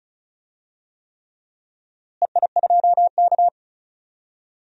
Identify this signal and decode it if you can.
Morse code, heard as a keyed tone, about 36 wpm, 700 Hz. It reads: EI2K
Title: DXCC-100 Call Signs - 4 Characters - Repeated 3x - 35wpm